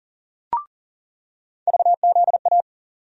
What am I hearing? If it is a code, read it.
Morse code, heard as a keyed tone, about 38 wpm, 700 Hz. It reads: VZA
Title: USA Call Sign Postfixes - 3 Letters 40wpm